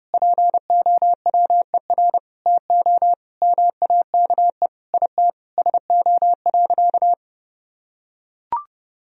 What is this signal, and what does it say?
Morse code, heard as a keyed tone, about 32 wpm, 700 Hz. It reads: POWER TO MAKE IT SO.
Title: Sentences from Top 1000 Words - Review Part 3 - 30wpm